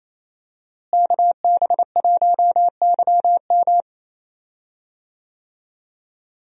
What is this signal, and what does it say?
Morse code, heard as a keyed tone, about 28 wpm, 700 Hz. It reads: KB1YM
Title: USA Call Signs - 5 Characters - Repeated 2x - 28wpm